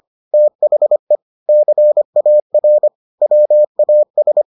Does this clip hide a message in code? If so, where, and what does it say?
Yes: Morse code from 0.3 to 4.4 s, heard as a keyed tone, about 25 wpm, 600 Hz. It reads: THE CAR WAS